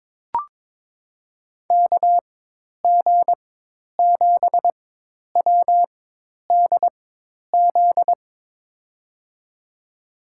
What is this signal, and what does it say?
Morse code, heard as a keyed tone, about 22 wpm, spaced as for 10 wpm, 700 Hz. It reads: KG7WDZ
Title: USA Call Signs - 6 Characters - 4x Character Spacing - 22wpm